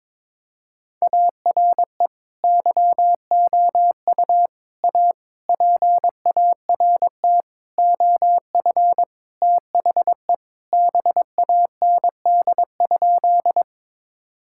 Morse code, heard as a keyed tone, about 22 wpm, 700 Hz. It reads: ARE YOU A PART OF THE BAND?
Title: Sentences from Top 1600 Words 22wpm